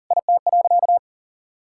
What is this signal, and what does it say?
Morse code, heard as a keyed tone, about 40 wpm, 700 Hz. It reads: IT.
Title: Sentences from Top 1000 Words - Review Part 5 - 40wpm